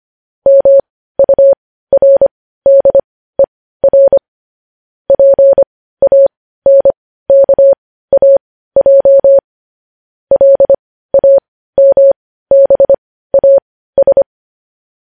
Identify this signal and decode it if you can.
Morse code, heard as a keyed tone, about 25 wpm, spaced as for 15 wpm, 550 Hz. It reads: MURDER PANKAJ LAMBAS
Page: news-headlines-in-morse-code-at-15-wpm